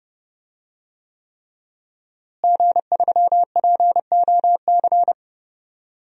Morse code, heard as a keyed tone, about 31 wpm, 700 Hz. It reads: G3POC
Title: DXCC-100 Call Signs - 5 Characters - Repeated 3x - 30wpm